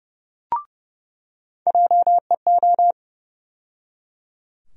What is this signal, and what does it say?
Morse code, heard as a keyed tone, about 30 wpm, 700 Hz. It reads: JEO
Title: USA Call Sign Postfixes - 3 Letters 30wpm